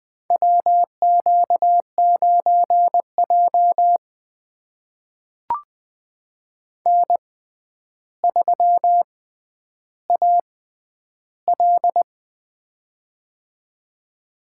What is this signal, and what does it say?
Morse code, heard as a keyed tone, about 20 wpm, spaced as for 10 wpm, 700 Hz. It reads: WQ9J N3AL
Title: USA Call Signs - 6x Character Spacing - 20wpm